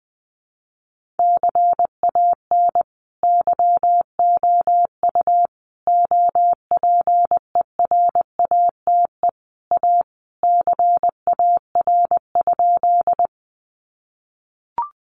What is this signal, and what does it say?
Morse code, heard as a keyed tone, about 20 wpm, 700 Hz. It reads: CAN YOU OPERATE A CAR?
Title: Sentences from Top 1500 Words 20wpm